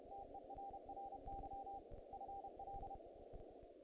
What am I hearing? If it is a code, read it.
Morse code, heard as a keyed tone, about 31 wpm, 790 Hz, with noise. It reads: TERRY FL
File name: contests / naqp-cw-k5tr-op-k5pi- -1900